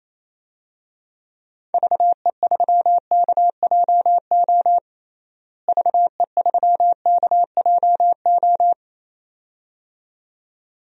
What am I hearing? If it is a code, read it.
Morse code, heard as a keyed tone, about 28 wpm, 700 Hz. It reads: VE3KJO VE3KJO
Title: DXCC-100 Call Signs - 6 Characters - Repeated 2x - 28wpm